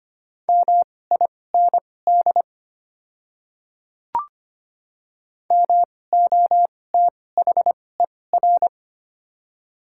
Morse code, heard as a keyed tone, about 25 wpm, spaced as for 18 wpm, 700 Hz. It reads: MIND MOTHER
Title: 2x Character Spacing - Top 500 Words - 25wpm